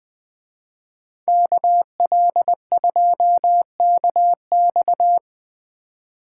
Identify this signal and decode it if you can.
Morse code, heard as a keyed tone, about 20 wpm, 700 Hz. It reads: KL2KX
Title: Speed Racing - USA 2x2 Call Signs 17wpm